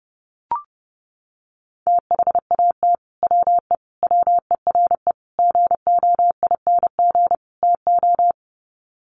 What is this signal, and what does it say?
Morse code, heard as a keyed tone, about 30 wpm, 700 Hz. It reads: THAT WE WERE GOING TO
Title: Sets of 5 Words Encore2 30wpm